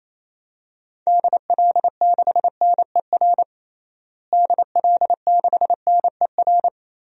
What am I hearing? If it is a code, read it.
Morse code, heard as a keyed tone, about 28 wpm, 700 Hz. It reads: DL6NER DL6NER
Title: DXCC-100 Call Signs - 6 Characters - Repeated 3x - 28wpm